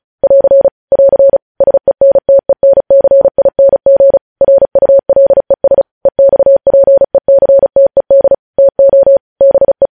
Morse code, heard as a keyed tone, about 36 wpm, 550 Hz. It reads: <AR> <AR> SENTENCING RULES EXPECTED TO BE
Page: News Headlines in Morse Code at 35 WPM